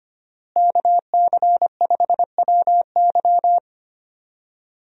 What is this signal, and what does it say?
Morse code, heard as a keyed tone, about 25 wpm, 700 Hz. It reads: KC5WY
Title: USA Call Signs - 5 Characters - Repeated 3x - 25wpm